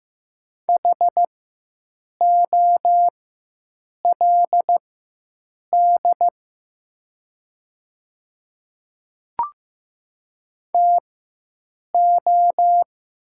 Morse code, heard as a keyed tone, about 15 wpm, spaced as for 7 wpm, 690 Hz. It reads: HOLD TO
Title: 4x Character Spacing - Top 500 Words - 15wpm